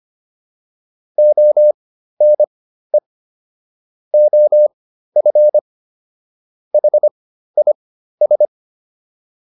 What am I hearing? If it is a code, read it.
Morse code, heard as a keyed tone, about 25 wpm, spaced as for 13 wpm, 600 Hz. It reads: ONE OF HIS